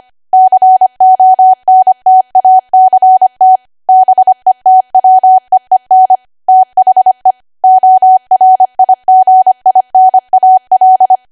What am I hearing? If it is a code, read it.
Morse code, heard as a keyed tone, about 25 wpm, 750 Hz. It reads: CONTACT BETWEEN THE ORIGINAL
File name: Archive / 25 WPM / 25WPM